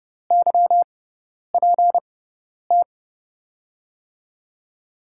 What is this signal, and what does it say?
Morse code, heard as a keyed tone, about 30 wpm, spaced as for 10 wpm, 700 Hz. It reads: YPT
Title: USA Call Signs - 6x Character Spacing - 30wpm